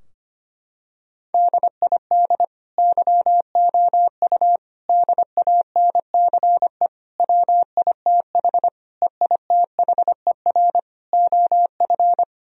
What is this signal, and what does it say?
Morse code, heard as a keyed tone, about 25 wpm, 700 Hz. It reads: DID YOU DANCE WITH EITHER OF
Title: Sentences from Top 1000 Words - Review Part 4 - 25wpm